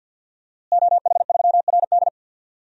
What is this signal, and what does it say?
Morse code, heard as a keyed tone, about 49 wpm, 700 Hz. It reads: OH3RD
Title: DXCC-100 Call Signs - 5 Characters - Repeated 3x - 50wpm